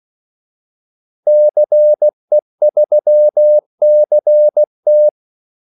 Morse code, heard as a keyed tone, about 16 wpm, 600 Hz. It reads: CE3CT
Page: Kaldesignaler 16 wpm